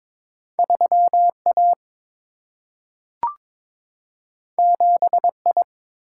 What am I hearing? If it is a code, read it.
Morse code, heard as a keyed tone, about 22 wpm, 700 Hz. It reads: 3A 7I
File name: practice / 22wpm / Two Letter-Number Combinations - 22wpm